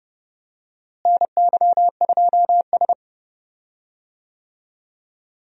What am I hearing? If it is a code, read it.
Morse code, heard as a keyed tone, about 32 wpm, 700 Hz. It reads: NY2S
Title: USA Call Signs - 4 Characters - Repeated 2x - 30wpm